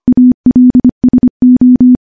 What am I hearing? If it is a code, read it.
Morse code, heard as a keyed tone, about 25 wpm, 250 Hz. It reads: ALSO
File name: morse-training-25WPM